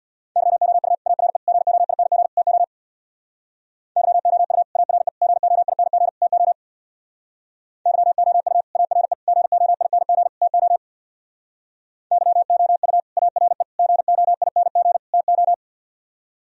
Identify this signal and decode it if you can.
Morse code, heard as a keyed tone, about 63 wpm, 700 Hz. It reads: YOU ARE GOING TO YOU ARE GOING TO YOU ARE GOING TO YOU ARE GOING TO